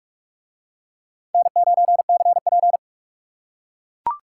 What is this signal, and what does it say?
Morse code, heard as a keyed tone, about 41 wpm, 700 Hz. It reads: N9CP